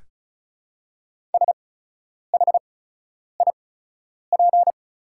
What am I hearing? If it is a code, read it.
Morse code, heard as a keyed tone, about 36 wpm, spaced as for 10 wpm, 700 Hz. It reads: SHIP
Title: 8x Character Spacing - Top 500 Words - 35wpm